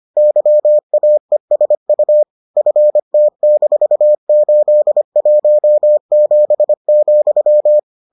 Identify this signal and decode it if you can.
Morse code, heard as a keyed tone, about 25 wpm, 600 Hz. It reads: YAESU FT-817,